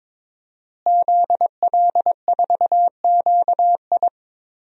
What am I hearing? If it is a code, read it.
Morse code, heard as a keyed tone, about 21 wpm, 700 Hz. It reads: ZL4QI
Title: DXCC-100 Call Signs - 5 Characters - Repeated 3x - 22wpm